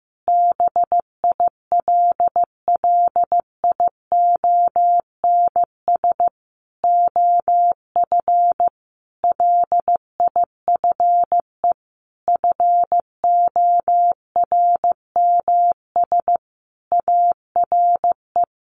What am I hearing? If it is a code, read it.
Morse code, heard as a keyed tone, about 15 wpm, 700 Hz. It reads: BILLIONS OF LIFE FORMS ARE
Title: A Sound Of Thunder 15wpm